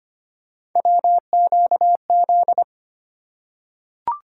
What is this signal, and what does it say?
Morse code, heard as a keyed tone, about 25 wpm, 700 Hz. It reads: WQZ
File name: Three Letter-Number Combinations - 25wpm